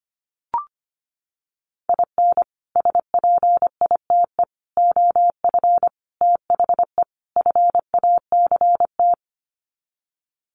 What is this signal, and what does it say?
Morse code, heard as a keyed tone, about 25 wpm, 700 Hz. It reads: IN SPITE OF THE FACT
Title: Sets of 5 Words Encore2 25wpm